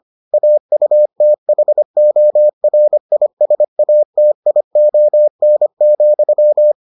Morse code, heard as a keyed tone, about 25 wpm, 600 Hz. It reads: AUTHORISATION,